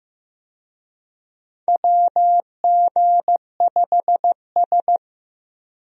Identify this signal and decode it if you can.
Morse code, heard as a keyed tone, about 16 wpm, 700 Hz. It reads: WG5S
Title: USA Call Signs - 4 Characters - Repeated 2x - 15wpm